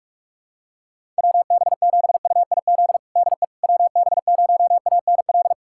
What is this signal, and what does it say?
Morse code, heard as a keyed tone, about 42 wpm, 700 Hz. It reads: WB7UIZ DE WB0ANL
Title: USA Callsign DE Callsign - 6 Characters - 45wpm